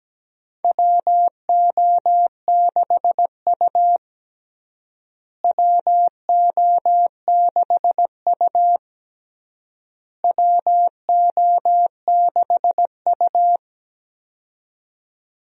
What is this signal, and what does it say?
Morse code, heard as a keyed tone, about 17 wpm, 700 Hz. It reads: WO6U WO6U WO6U
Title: USA Call Signs - 4 Characters - Repeated 3x - 17wpm